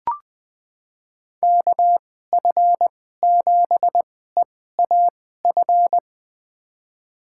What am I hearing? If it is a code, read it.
Morse code, heard as a keyed tone, about 20 wpm, spaced as for 15 wpm, 700 Hz. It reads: KF7EAF
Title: USA Call Signs - 6 Characters - 2x Character Spacing - 20wpm